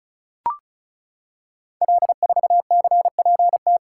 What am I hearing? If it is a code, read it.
Morse code, heard as a keyed tone, about 36 wpm, 700 Hz. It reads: L4CPT
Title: Code Groups - 5 Letter:Numbers - 35wpm